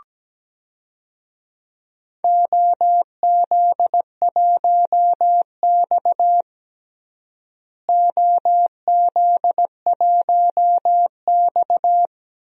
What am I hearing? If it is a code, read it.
Morse code, heard as a keyed tone, about 17 wpm, 700 Hz. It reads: OZ1X OZ1X
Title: DXCC-100 Call Signs - 4 Characters - Repeated 2x - 17wpm